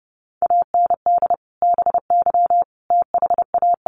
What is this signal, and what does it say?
Morse code, heard as a keyed tone, about 30 wpm, 700 Hz. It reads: AND BY THA
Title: Sentences from Top 1100 Words 30wpm